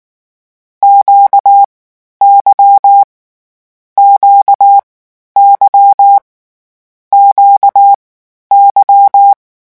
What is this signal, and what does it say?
Morse code, heard as a keyed tone, about 19 wpm, 800 Hz. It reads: Q Y Q Y Q Y
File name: morse mp3 / lecon2-vitesse 18